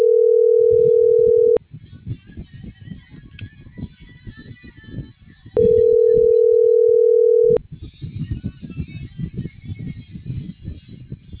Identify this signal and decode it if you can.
Ringback tone (North America), 440 + 480 Hz.